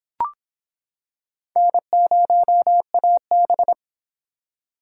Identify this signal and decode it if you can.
Morse code, heard as a keyed tone, about 26 wpm, 700 Hz. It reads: N0AB